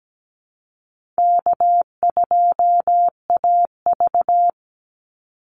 Morse code, heard as a keyed tone, about 17 wpm, 700 Hz. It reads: K2AV
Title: Easy Call Signs Repeated 3x - 17wpm